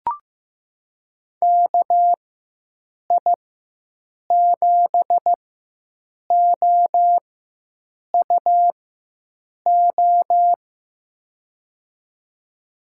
Morse code, heard as a keyed tone, about 15 wpm, spaced as for 7 wpm, 700 Hz. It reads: KI7OUO
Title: USA Call Signs - 6 Characters - 4x Character Spacing - 15wpm